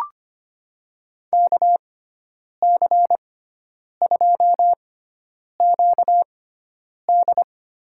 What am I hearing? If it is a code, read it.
Morse code, heard as a keyed tone, about 25 wpm, spaced as for 9 wpm, 700 Hz. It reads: KC2QD
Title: USA Call Signs - 6 Characters - 6x Character Spacing - 25wpm